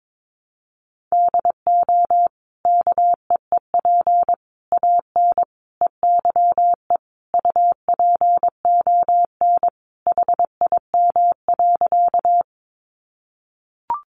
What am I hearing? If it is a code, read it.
Morse code, heard as a keyed tone, about 22 wpm, 700 Hz. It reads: DO KEEP AN EYE UPON HIM.
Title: Sentences from Top 1100 Words 22wpm